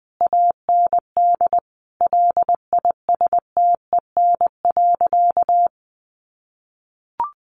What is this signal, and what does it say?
Morse code, heard as a keyed tone, about 20 wpm, 700 Hz. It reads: AND LISTEN.
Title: Sentences from Top 900 Words 20wpm